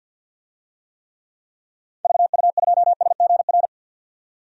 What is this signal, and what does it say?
Morse code, heard as a keyed tone, about 51 wpm, 700 Hz. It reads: VU2SGR
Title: DXCC-100 Call Signs - 6 Characters - 50wpm